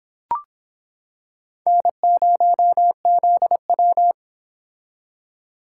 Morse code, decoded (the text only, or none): N0ZW